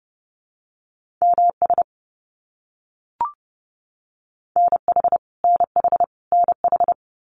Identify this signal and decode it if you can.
Morse code, heard as a keyed tone, about 30 wpm, 700 Hz. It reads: MS NH NH NH
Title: US States Repeated 3x 30wpm